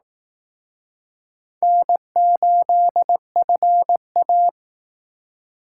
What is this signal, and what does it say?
Morse code, heard as a keyed tone, about 18 wpm, 700 Hz. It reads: N8FA
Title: Speed Racing - USA 1x2 Call Signs 15wpm